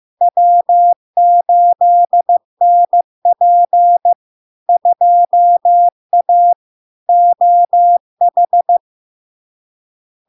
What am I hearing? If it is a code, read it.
Morse code, heard as a keyed tone, about 15 wpm, 700 Hz. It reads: W8NP 2A OH
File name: LARK-FD-2022-Log-in-CW-15wpm-700-Hz